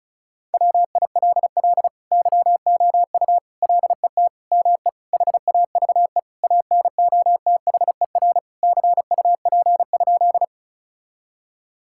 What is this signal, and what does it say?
Morse code, heard as a keyed tone, about 36 wpm, 700 Hz. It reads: WILL YOU LET ME HAVE ANOTHER CUP?
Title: Sentences from Top 1600 Words 35wpm